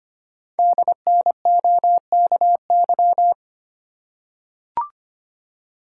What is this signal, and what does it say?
Morse code, heard as a keyed tone, about 25 wpm, 700 Hz. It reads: DNOKY
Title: Code Groups - 5 Letters - 25wpm